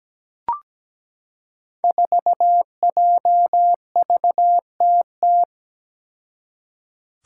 Morse code, heard as a keyed tone, about 17 wpm, 700 Hz. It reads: 4JVTT
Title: Five Challenging Letter-Number Combinations 17wpm